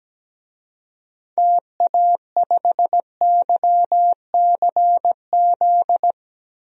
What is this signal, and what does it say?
Morse code, heard as a keyed tone, about 17 wpm, 700 Hz. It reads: TA5YCZ
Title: DXCC-100 Call Signs - 6 Characters - Repeated 3x - 17wpm